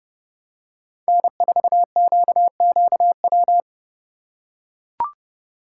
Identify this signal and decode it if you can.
Morse code, heard as a keyed tone, about 31 wpm, 700 Hz. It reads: N4QQW
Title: USA Call Signs - 5 Characters - 4x Character Spacing - 30wpm